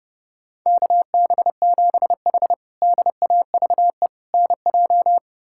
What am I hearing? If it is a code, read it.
Morse code, heard as a keyed tone, about 30 wpm, 700 Hz. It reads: KB7H DAVE NJ
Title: K1USN SST QSO Practice - 30wpm